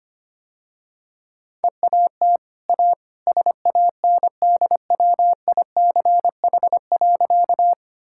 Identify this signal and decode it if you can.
Morse code, heard as a keyed tone, about 25 wpm, 700 Hz. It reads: EAT A SANDWICH.
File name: Easy Sentences with Hints - 25wpm